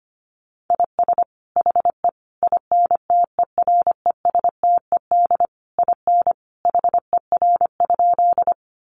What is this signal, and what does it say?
Morse code, heard as a keyed tone, about 25 wpm, 700 Hz. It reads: IS HE INTERESTED IN HER?
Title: Sentences from Top 900 Words 25wpm